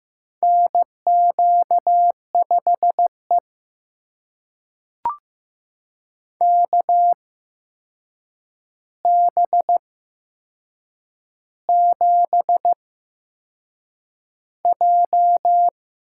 Morse code, heard as a keyed tone, about 15 wpm, spaced as for 6 wpm, 700 Hz. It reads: NQ5E KB7J
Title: USA Call Signs - 4 Characters - 8x Character Spacing - 15wpm